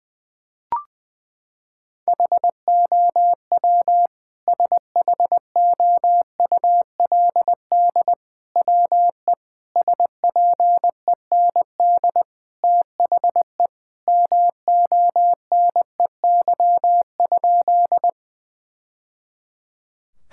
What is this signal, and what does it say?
Morse code, heard as a keyed tone, about 19 wpm, 700 Hz. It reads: HOW SHOULD WE SPEND THE MONEY?
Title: Sentences from Top 1000 Words - Review Part 5 - 20wpm